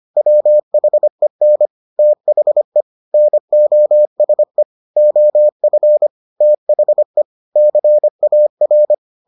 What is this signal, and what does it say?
Morse code, heard as a keyed tone, about 25 wpm, 600 Hz. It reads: WHEN THE NOSE OF THE CAR